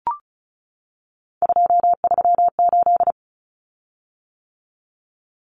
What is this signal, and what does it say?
Morse code, heard as a keyed tone, about 34 wpm, 700 Hz. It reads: 238
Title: Common Cut Numbers - 3-Digit 35wpm